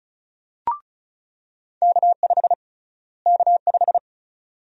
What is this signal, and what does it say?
Morse code, heard as a keyed tone, about 36 wpm, 700 Hz. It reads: K5 K5
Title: USA Call Sign Prefixes - Repeated 2x - 35wpm